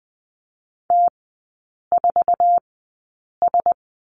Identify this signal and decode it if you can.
Morse code, heard as a keyed tone, about 20 wpm, spaced as for 8 wpm, 690 Hz. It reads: T4S
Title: 060 Lesson 11e - Letter h - Three Characters 20wpm